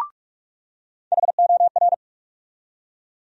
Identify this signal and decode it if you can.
Morse code, heard as a keyed tone, about 46 wpm, 700 Hz. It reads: HOR
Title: USA Call Sign Postfixes - 3 Letters 45wpm